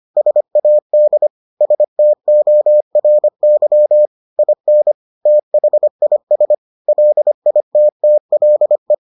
Morse code, heard as a keyed tone, about 25 wpm, 600 Hz. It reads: SAD STORY IN THIS LITTLE